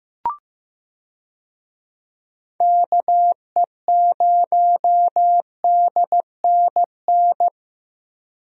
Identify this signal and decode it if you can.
Morse code, heard as a keyed tone, about 15 wpm, 700 Hz. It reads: KE0DNN
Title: USA Call Signs - 6 Characters - Repeated 2x - 15wpm